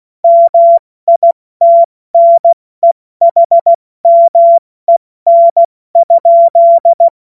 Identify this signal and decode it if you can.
Morse code, heard as a keyed tone, about 16 wpm, spaced as for 14 wpm, 680 Hz. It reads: MITNEHMEN?
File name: Morsekurs / Witze / Witz 27. Januar 2026 70 F80